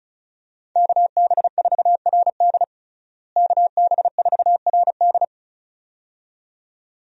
Morse code, heard as a keyed tone, about 35 wpm, 700 Hz. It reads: KB4RD KB4RD
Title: USA Call Signs - 5 Characters - Repeated 2x - 35wpm